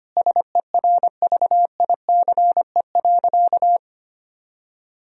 Morse code, heard as a keyed tone, about 25 wpm, 700 Hz. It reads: SERVICE.
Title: Sentences from Top 1000 Words - Review Part 4 - 25wpm